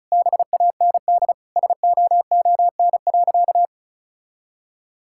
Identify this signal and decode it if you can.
Morse code, heard as a keyed tone, about 36 wpm, 700 Hz. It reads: BAND SOON.
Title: Sentences from Top 1600 Words 35wpm